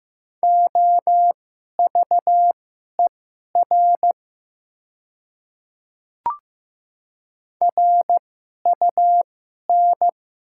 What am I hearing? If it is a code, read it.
Morse code, heard as a keyed tone, about 15 wpm, spaced as for 11 wpm, 700 Hz. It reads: OVER RUN
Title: 2x Character Spacing - Top 300 Words - 15wpm